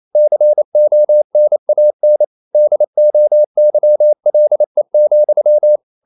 Morse code, heard as a keyed tone, about 28 wpm, 600 Hz. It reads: CONAN DOYLE,
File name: CW / Practice / EnglishQuotations0228